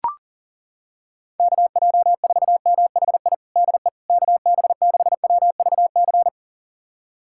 Morse code, heard as a keyed tone, about 40 wpm, 700 Hz. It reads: KJ4MHI DE KB6WVC
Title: USA Callsign DE Callsign - 6 Characters - 40wpm